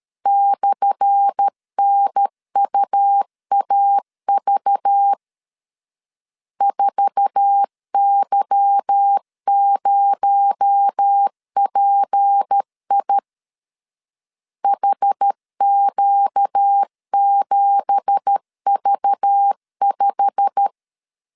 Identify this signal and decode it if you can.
Morse code, heard as a keyed tone, about 13 wpm, 800 Hz. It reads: /NUAV 4Y0PI HQ7V5
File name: LEZIONI / 10 SIGNS / 10A SLASH 12wpm 2314